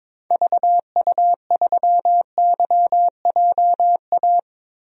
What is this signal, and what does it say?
Morse code, heard as a keyed tone, about 21 wpm, 700 Hz. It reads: VU3YJA
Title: DXCC-100 Call Signs - 6 Characters - Repeated 2x - 22wpm